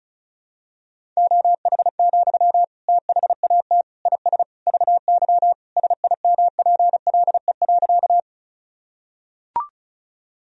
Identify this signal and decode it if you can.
Morse code, heard as a keyed tone, about 36 wpm, 700 Hz. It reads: OH, THAT IS VY SIMPLE.
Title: Sentences from Top 500 Words and Abbreviations 35wpm